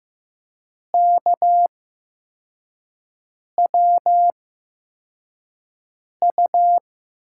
Morse code, heard as a keyed tone, about 15 wpm, spaced as for 4 wpm, 700 Hz. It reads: KWU
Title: USA Call Signs - 5 Characters - 8x Character Spacing - 15wpm